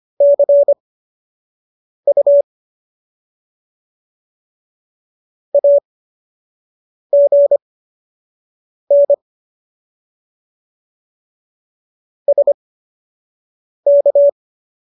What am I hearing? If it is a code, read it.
Morse code, heard as a keyed tone, about 25 wpm, spaced as for 6 wpm, 570 Hz. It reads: CU AGN SK